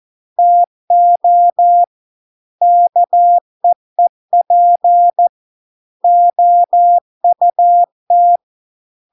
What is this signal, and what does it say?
Morse code, heard as a keyed tone, about 14 wpm, 700 Hz. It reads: TO KEEP OUT